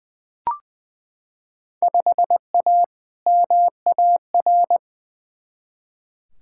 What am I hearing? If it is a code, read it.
Morse code, heard as a keyed tone, about 20 wpm, 700 Hz. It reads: 5A MAR
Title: Field Day Exchanges Encore - 20wpm